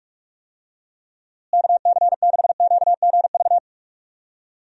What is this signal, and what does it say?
Morse code, heard as a keyed tone, about 46 wpm, 700 Hz. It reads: KC6QGV
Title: USA Call Signs - 6 Characters - Repeated 3x - 45wpm